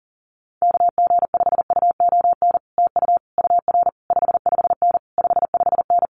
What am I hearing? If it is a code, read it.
Morse code, heard as a keyed tone, about 40 wpm, 700 Hz. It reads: KG5UON TU UR 55N 55N
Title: Common QSO Phrases Encore 40wpm